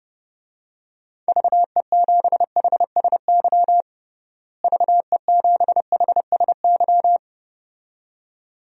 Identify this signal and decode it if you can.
Morse code, heard as a keyed tone, about 29 wpm, 700 Hz. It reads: VE7HSY VE7HSY
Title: DXCC-100 Call Signs - 6 Characters - Repeated 2x - 30wpm